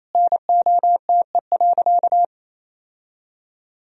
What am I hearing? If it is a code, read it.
Morse code, heard as a keyed tone, about 28 wpm, 700 Hz. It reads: NOTE.
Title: Sentences from Top 1000 Words - Review Part 2 - 28wpm